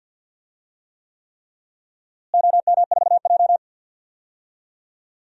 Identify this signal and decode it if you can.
Morse code, heard as a keyed tone, about 52 wpm, 700 Hz. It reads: OM4J
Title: DXCC-100 Call Signs - 4 Characters - Repeated 3x - 50wpm